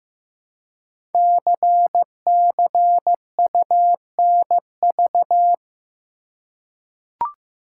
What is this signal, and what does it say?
Morse code, heard as a keyed tone, about 15 wpm, 700 Hz. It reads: CCUNV
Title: Code Groups - 5 Letters - 15wpm